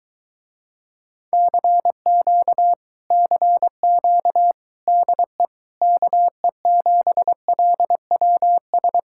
Morse code, heard as a keyed tone, about 23 wpm, 700 Hz. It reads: CQ CQ DE KE7LWS